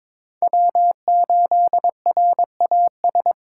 Morse code, heard as a keyed tone, about 22 wpm, 700 Hz. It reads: W8RAS
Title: USA Call Signs - 5 Characters - 2x Character Spacing - 22wpm